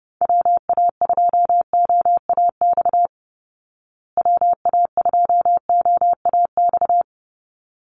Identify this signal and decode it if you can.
Morse code, heard as a keyed tone, about 30 wpm, 700 Hz. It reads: WA2OAX WA2OAX
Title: Call Signs Repeated 3x - 30wpm